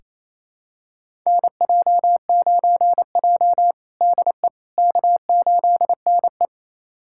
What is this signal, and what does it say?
Morse code, heard as a keyed tone, about 28 wpm, 700 Hz. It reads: NJ9J DE K8NE
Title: USA Callsign DE Callsign - 4 Characters - 28wpm